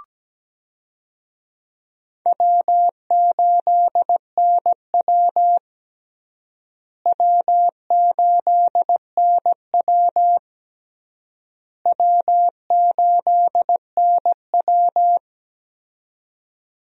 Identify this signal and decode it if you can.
Morse code, heard as a keyed tone, about 17 wpm, 700 Hz. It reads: W8NW W8NW W8NW
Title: USA Call Signs - 4 Characters - Repeated 3x - 17wpm